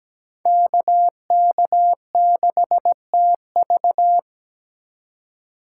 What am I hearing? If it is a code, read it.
Morse code, heard as a keyed tone, about 17 wpm, 700 Hz. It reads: KK6TV
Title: K1USN SST QSO Practice - 17wpm